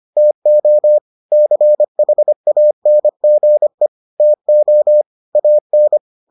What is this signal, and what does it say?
Morse code, heard as a keyed tone, about 25 wpm, 600 Hz. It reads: TO CHANGE TO AN